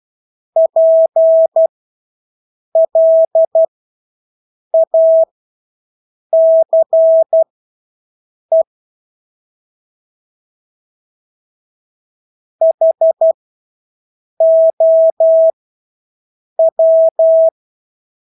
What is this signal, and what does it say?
Morse code, heard as a keyed tone, about 12 wpm, spaced as for 6 wpm, 650 Hz. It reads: PLACE HOW
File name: Engelske ord 12-6 wpm